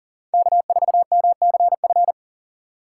Morse code, heard as a keyed tone, about 40 wpm, 700 Hz. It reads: K4MCF
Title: USA Call Signs - 5 Characters - 4x Character Spacing - 40wpm